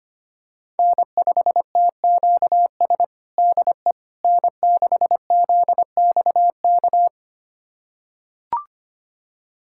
Morse code, heard as a keyed tone, about 25 wpm, 700 Hz. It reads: N5TQS DE N6ZXK
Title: USA Callsign DE Callsign - 5 Characters - 25wpm